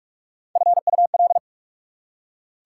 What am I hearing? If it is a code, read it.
Morse code, heard as a keyed tone, about 44 wpm, 690 Hz. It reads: FUL